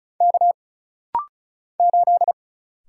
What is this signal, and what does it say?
Morse code, heard as a keyed tone, about 34 wpm, 700 Hz. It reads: K 8
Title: Single Letter-Number - Rapid-Fire 35wpm